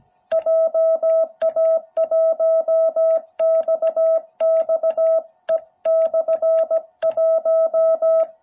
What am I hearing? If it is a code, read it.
Morse code, heard as a keyed tone, about 17 wpm, 660 Hz. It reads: JA1XXE/1